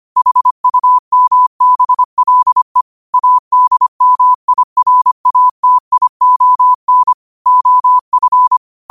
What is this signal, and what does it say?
Morse code, heard as a keyed tone, about 25 wpm, 1000 Hz. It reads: HUMBLE ADMIRATION OF
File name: qotdfiles / 25wpm